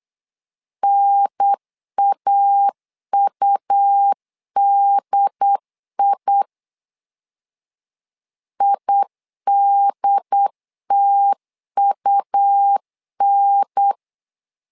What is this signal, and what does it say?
Morse code, heard as a keyed tone, about 8 wpm, 800 Hz. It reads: NAUDI IDTUN
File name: LEZIONI / 3 MWRO / 3 M 8wpm 2300